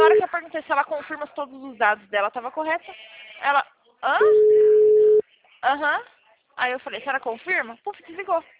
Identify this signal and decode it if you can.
Ringback tone (Europe, 425 Hz), with speech.